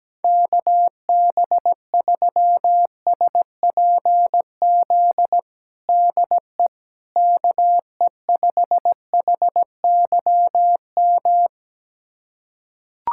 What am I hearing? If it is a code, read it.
Morse code, heard as a keyed tone, about 17 wpm, 700 Hz. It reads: KB3SPZ DE KE5HYM
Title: USA Callsign DE Callsign - 6 Characters - 17wpm